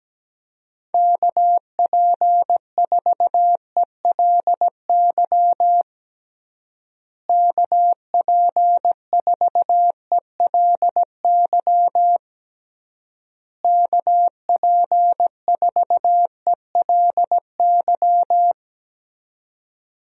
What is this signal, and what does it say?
Morse code, heard as a keyed tone, about 17 wpm, 700 Hz. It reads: KP4ELY KP4ELY KP4ELY
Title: USA Call Signs - 6 Characters - Repeated 3x - 17wpm